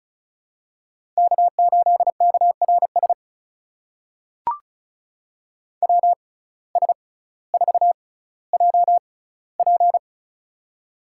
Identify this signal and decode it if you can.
Morse code, heard as a keyed tone, about 36 wpm, spaced as for 18 wpm, 700 Hz. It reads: K8KRS WS4JP
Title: USA Call Signs - 5 Characters - 6x Character Spacing - 35wpm